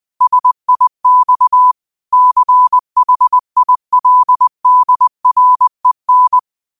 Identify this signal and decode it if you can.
Morse code, heard as a keyed tone, about 20 wpm, 1000 Hz. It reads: SIX CHILDREN
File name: qotdfiles / 20wpm